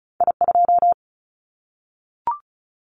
Morse code, heard as a keyed tone, about 36 wpm, 700 Hz. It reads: I2
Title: Callsign Prefixes for Top 25 DXCC - 35wpm